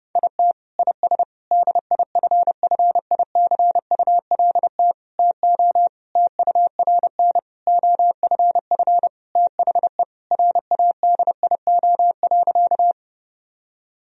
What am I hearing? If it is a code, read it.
Morse code, heard as a keyed tone, about 30 wpm, 700 Hz. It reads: IT IS DIFFICULT TO TURN OFF THE RADIO.
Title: Sentences from Top 1000 Words - Review Part 3 - 30wpm